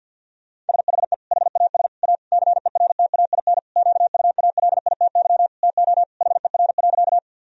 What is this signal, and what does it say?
Morse code, heard as a keyed tone, about 47 wpm, 700 Hz. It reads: SHE HAS A CERTAIN QUALITY TO HER.
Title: Sentences from Top 1000 Words - Review Part 4 - 50wpm